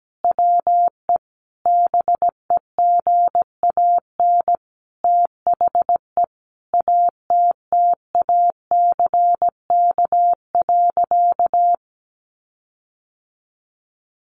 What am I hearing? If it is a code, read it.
Morse code, heard as a keyed tone, about 17 wpm, 700 Hz. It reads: WE BEGAN THE ATTACK.
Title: Sentences from Top 1500 Words 17wpm